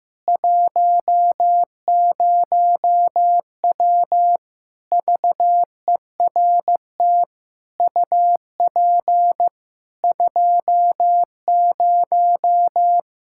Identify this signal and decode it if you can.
Morse code, heard as a keyed tone, about 16 wpm, 700 Hz. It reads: 10W VERT UP 20
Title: Practice QSO - Exchange 2 - Info - 15wpm